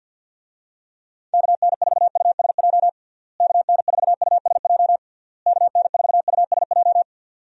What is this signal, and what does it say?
Morse code, heard as a keyed tone, about 50 wpm, 700 Hz. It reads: KN4USJ KN4USJ KN4USJ
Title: USA Call Signs - 6 Characters - Repeated 3x - 50wpm